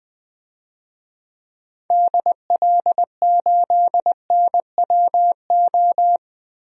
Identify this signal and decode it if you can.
Morse code, heard as a keyed tone, about 20 wpm, 700 Hz. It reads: DL8NWO
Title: DXCC-100 Call Signs - 6 Characters - 20wpm